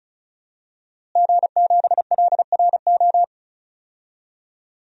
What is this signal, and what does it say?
Morse code, heard as a keyed tone, about 36 wpm, 700 Hz. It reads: G7LRO
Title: DXCC-100 Call Signs - 5 Characters - Repeated 3x - 35wpm